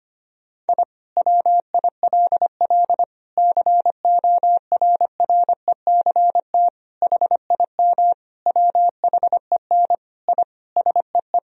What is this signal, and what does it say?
Morse code, heard as a keyed tone, about 25 wpm, 700 Hz. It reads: I WILL CORRECT HIM WHEN I SEE
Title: Sentences from Top 1000 Words - Review Part 2 - 25wpm